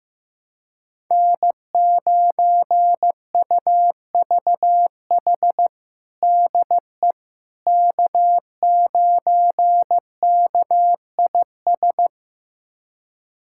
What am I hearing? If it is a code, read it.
Morse code, heard as a keyed tone, about 15 wpm, 700 Hz. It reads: N9UVH DE K9KIS
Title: USA Callsign DE Callsign - 5 Characters - 15wpm